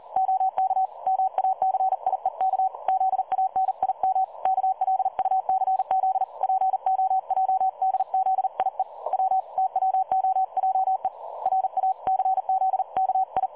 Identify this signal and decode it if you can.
Morse code, heard as a keyed tone, about 40 wpm, 760 Hz. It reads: OK MACIEK ZANIM KLUCZ POJDZIE W TWOJE RACZKI
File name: dlazaawansowanych / nagraniazpasma / websdr recording 3565.0kHz